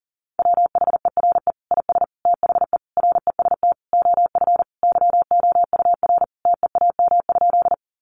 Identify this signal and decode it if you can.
Morse code, heard as a keyed tone, about 40 wpm, 700 Hz. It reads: WHERE IS THE REST OF YOUR TEAM?
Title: Sentences from Top 1500 Words 40wpm